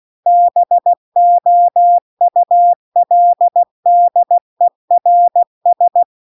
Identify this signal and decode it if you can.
Morse code, heard as a keyed tone, about 16 wpm, 700 Hz. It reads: BOULDERS
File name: CW-The-Big-Twoheaded-River-2-16WPM